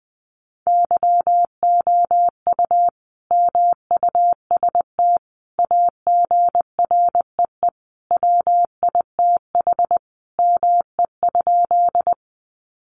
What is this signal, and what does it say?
Morse code, heard as a keyed tone, about 20 wpm, 700 Hz. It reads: YOU MUST AGREE WITH ME?
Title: Sentences from Top 1100 Words 20wpm